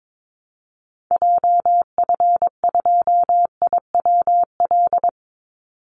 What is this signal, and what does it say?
Morse code, heard as a keyed tone, about 22 wpm, 700 Hz. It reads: JF2IWL
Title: Call Signs Repeated 3x - 22wpm